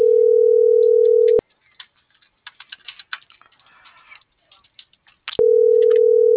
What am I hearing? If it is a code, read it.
Ringback tone (North America), 440 + 480 Hz.
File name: CENTRAL MULTIMODAL / out angel.velazquez